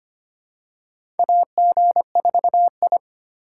Morse code, heard as a keyed tone, about 25 wpm, 700 Hz. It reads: AG4I
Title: USA Call Signs - 4 Characters - 4x Character Spacing - 25wpm